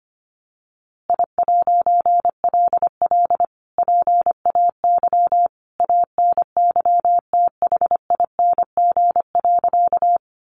Morse code, heard as a keyed tone, about 25 wpm, 700 Hz. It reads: I'LL PAY ANYTHING.
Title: A Sound Of Thunder 25wpm